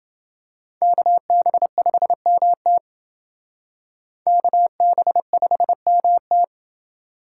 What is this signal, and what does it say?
Morse code, heard as a keyed tone, about 28 wpm, 700 Hz. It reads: KB5MT KB5MT